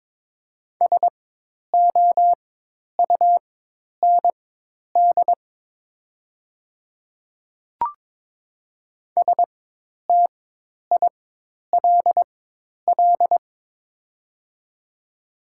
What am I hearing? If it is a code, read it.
Morse code, heard as a keyed tone, about 22 wpm, spaced as for 10 wpm, 700 Hz. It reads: SOUND STILL
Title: 4x Character Spacing - Top 500 Words - 22wpm